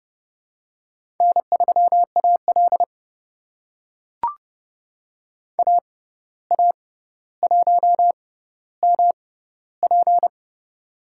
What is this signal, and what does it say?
Morse code, heard as a keyed tone, about 30 wpm, spaced as for 14 wpm, 700 Hz. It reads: N3AL AA1MP
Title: USA Call Signs - 6x Character Spacing - 30wpm